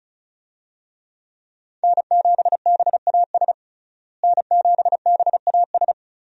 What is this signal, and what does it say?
Morse code, heard as a keyed tone, about 36 wpm, 700 Hz. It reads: N7BAS N7BAS
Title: USA Call Signs - 5 Characters - Repeated 2x - 35wpm